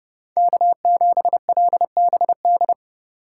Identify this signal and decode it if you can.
Morse code, heard as a keyed tone, about 31 wpm, 700 Hz. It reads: K7LBD